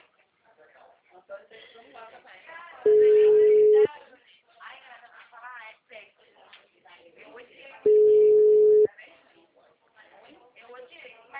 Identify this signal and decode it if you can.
Ringback tone (Europe, 425 Hz), with speech.